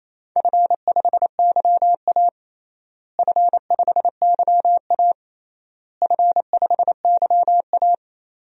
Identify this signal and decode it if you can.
Morse code, heard as a keyed tone, about 27 wpm, 700 Hz. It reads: F5YA F5YA F5YA
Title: DXCC-100 Call Signs - 4 Characters - Repeated 3x - 28wpm